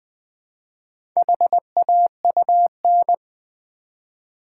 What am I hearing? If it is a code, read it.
Morse code, heard as a keyed tone, about 20 wpm, 700 Hz. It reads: HAUN
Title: Build A Word - 1 Letter At A Time - 3 to 5-Letter Words - 20wpm